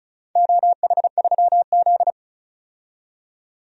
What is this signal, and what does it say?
Morse code, heard as a keyed tone, about 36 wpm, 690 Hz. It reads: OH3Z
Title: DXCC-100 Call Signs - 4 Characters - 35wpm